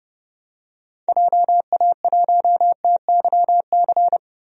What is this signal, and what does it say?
Morse code, heard as a keyed tone, about 30 wpm, 700 Hz. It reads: JA1TYC
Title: DXCC-100 Call Signs - 6 Characters - Repeated 2x - 30wpm